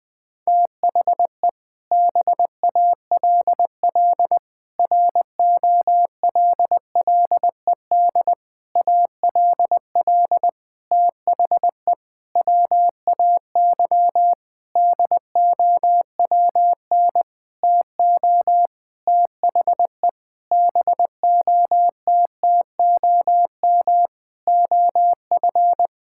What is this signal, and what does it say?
Morse code, heard as a keyed tone, about 20 wpm, 700 Hz. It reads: THE BALL ROLLED ALL THE WAY DOWN TO THE BOTTOM OF
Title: Sentences with 6-Word Phrases - 20wpm